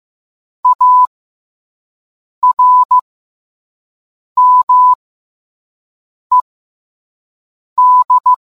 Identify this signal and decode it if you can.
Morse code, heard as a keyed tone, about 15 wpm, spaced as for 5 wpm, 1000 Hz. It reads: ARMED